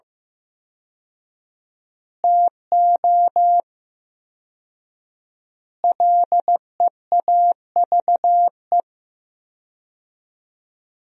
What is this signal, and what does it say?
Morse code, heard as a keyed tone, about 16 wpm, 700 Hz. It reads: TO LEAVE
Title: 4x Word-Spacing - Sentences from Top 400 Words - 15wpm